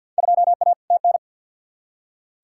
Morse code, heard as a keyed tone, about 47 wpm, 700 Hz. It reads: 2A TN
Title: Field Day Exchanges - 50wpm